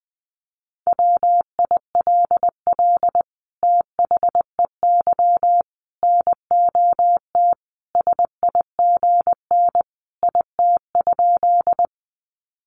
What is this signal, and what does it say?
Morse code, heard as a keyed tone, about 20 wpm, 700 Hz. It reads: WILL THEY NOT SIGN IT?
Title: Sentences from Top 1000 Words 20wpm